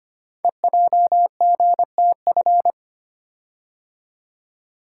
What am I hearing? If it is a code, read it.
Morse code, heard as a keyed tone, about 25 wpm, 700 Hz. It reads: EJGTF
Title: Code Groups - 5 Letters - 25wpm